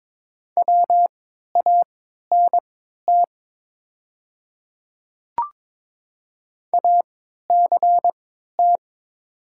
Morse code, heard as a keyed tone, about 22 wpm, spaced as for 12 wpm, 700 Hz. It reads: WANT ACT
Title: 3x Character Spacing - Top 200 Words - 22wpm